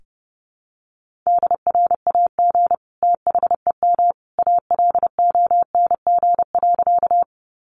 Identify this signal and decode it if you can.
Morse code, heard as a keyed tone, about 30 wpm, 700 Hz. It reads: DRAG THEM ALONG.
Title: Sentences from Top 1500 Words 30wpm